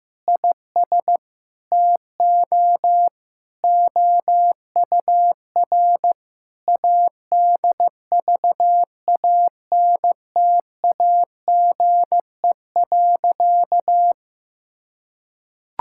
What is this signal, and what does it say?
Morse code, heard as a keyed tone, about 15 wpm, 700 Hz. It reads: IS TO OUR ADVANTAGE.
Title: Sentences from Top 1000 Words - Review Part 6 - 15wpm